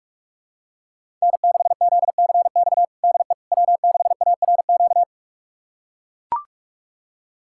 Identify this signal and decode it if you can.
Morse code, heard as a keyed tone, about 42 wpm, 700 Hz. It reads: N6ZCX DE W6ARQ
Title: USA Callsign DE Callsign - 5 Characters - 45wpm